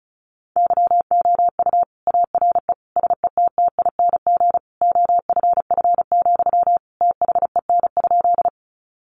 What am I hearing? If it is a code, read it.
Morse code, heard as a keyed tone, about 34 wpm, 700 Hz. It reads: YOU ARE SETTING OFF, THEN?
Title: Sentences from Top 1300 Words 35wpm